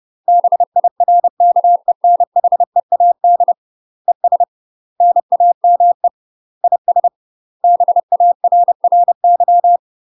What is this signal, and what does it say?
Morse code, heard as a keyed tone, about 30 wpm, 700 Hz. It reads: BIRKENHEAD ES NAME IS BARRY
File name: UK MASTER QSOs TRAINING / 30WPM / UK MASTER QSOs 0045